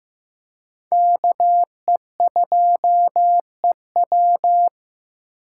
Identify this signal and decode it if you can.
Morse code, heard as a keyed tone, about 15 wpm, 700 Hz. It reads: KE2EW
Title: USA Call Signs - 8x Character Spacing - 15wpm